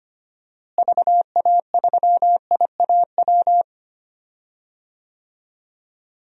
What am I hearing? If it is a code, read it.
Morse code, heard as a keyed tone, about 25 wpm, 700 Hz. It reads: VA3IAW
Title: DXCC-100 Call Signs - 6 Characters - Repeated 3x - 25wpm